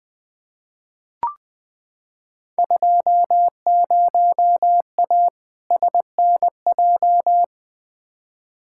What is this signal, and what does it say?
Morse code, heard as a keyed tone, about 20 wpm, 700 Hz. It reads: 20A SNJ
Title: Field Day Exchanges Encore - 20wpm